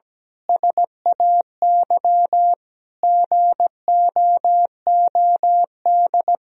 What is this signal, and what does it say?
Morse code, heard as a keyed tone, about 17 wpm, 700 Hz. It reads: SAY GOOD
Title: Sentences from Top 1000 Words - Review Part 4 - 17wpm